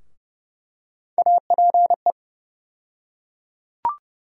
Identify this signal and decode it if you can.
Morse code, heard as a keyed tone, about 30 wpm, 700 Hz. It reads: APE
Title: USA Call Sign Postfixes - 3 Letters 30wpm